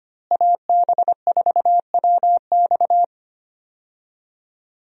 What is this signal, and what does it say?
Morse code, heard as a keyed tone, about 25 wpm, 700 Hz. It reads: AB4WX
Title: USA Call Signs - 5 Characters - Repeated 3x - 25wpm